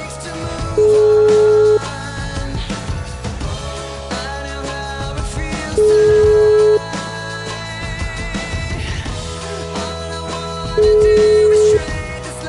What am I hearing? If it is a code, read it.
Ringback tone (Europe, 425 Hz).